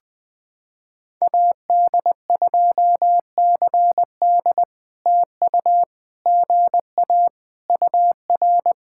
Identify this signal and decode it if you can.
Morse code, heard as a keyed tone, about 20 wpm, 700 Hz. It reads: AD2CD TU GA UR
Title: POTA QSO Practice - 20wpm